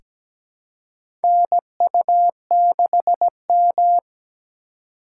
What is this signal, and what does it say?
Morse code, heard as a keyed tone, about 17 wpm, 700 Hz. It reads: NU6M
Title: USA Call Signs - 2x Character Spacing - 17wpm